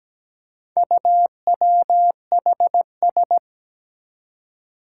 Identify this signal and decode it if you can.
Morse code, heard as a keyed tone, about 17 wpm, 700 Hz. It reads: UWHS
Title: Four Challenging Letter-Number Combinations 17wpm